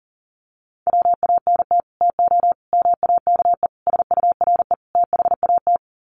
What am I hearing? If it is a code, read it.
Morse code, heard as a keyed tone, about 40 wpm, 700 Hz. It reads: WANT TO MAKE SURE THAT
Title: Sets of 5 Words 40wpm